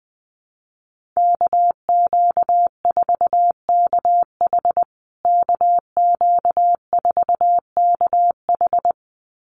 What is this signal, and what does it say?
Morse code, heard as a keyed tone, about 19 wpm, 700 Hz. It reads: KQ4KH KQ4KH
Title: USA Call Signs Repeated 2x 20wpm